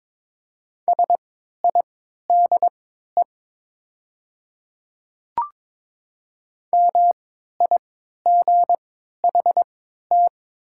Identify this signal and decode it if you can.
Morse code, heard as a keyed tone, about 23 wpm, spaced as for 13 wpm, 700 Hz. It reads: SIDE MIGHT
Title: 3x Character Spacing - Top 300 Words - 22wpm